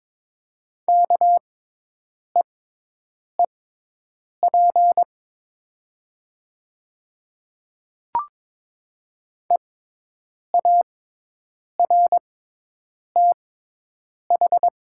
Morse code, heard as a keyed tone, about 22 wpm, spaced as for 8 wpm, 700 Hz. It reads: KEEP EARTH
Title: 6x Character Spacing - Top 500 Words - 22wpm